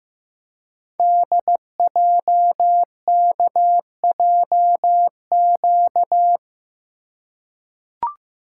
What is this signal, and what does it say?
Morse code, heard as a keyed tone, about 15 wpm, 700 Hz. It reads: DJKJQ
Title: Code Groups - 5 Letters - 15wpm